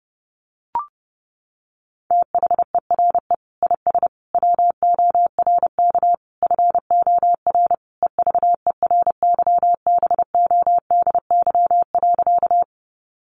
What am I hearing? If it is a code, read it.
Morse code, heard as a keyed tone, about 30 wpm, 700 Hz. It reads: THERE IS WORK FOR EVERYBODY.
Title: Sentences from Top 1400 Words 30wpm